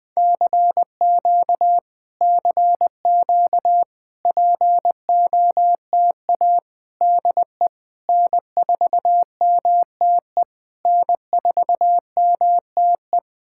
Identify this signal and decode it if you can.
Morse code, heard as a keyed tone, about 20 wpm, 700 Hz. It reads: CQ CQ POTA DE N4MTE N4MTE
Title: POTA QSO Practice - 20wpm